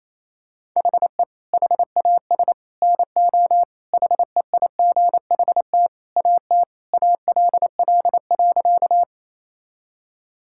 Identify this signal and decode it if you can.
Morse code, heard as a keyed tone, about 27 wpm, 700 Hz. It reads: HE HAS NO HEIGHT AT ALL.
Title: Sentences from Top 1600 Words 28wpm